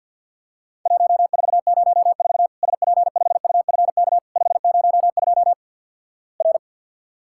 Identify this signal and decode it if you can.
Morse code, heard as a keyed tone, about 51 wpm, 700 Hz. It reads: 1404 SP5UFK 502 R
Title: Worked All Europe DX Contest - QTC Exchange - 50wpm